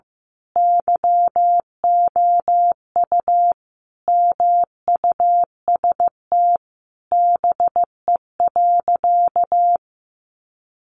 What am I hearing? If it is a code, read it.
Morse code, heard as a keyed tone, about 15 wpm, 700 Hz. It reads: YOU MUST BE.
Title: Sentences from Top 1400 Words 15wpm